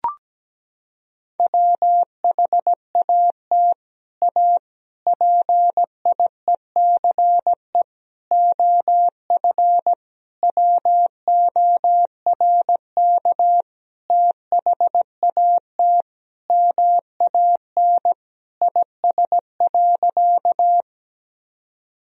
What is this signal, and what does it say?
Morse code, heard as a keyed tone, about 17 wpm, 700 Hz. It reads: WHAT A PIECE OF WORK THAT MAN IS.
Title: Sentences from Top 1000 Words - Review Part 2 - 17wpm